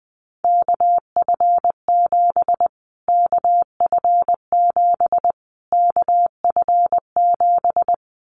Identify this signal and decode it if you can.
Morse code, heard as a keyed tone, about 19 wpm, 700 Hz. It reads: KF7 KF7 KF7
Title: Callsign Prefixes for Top 25 DXCC - Repeated 3x - 20wpm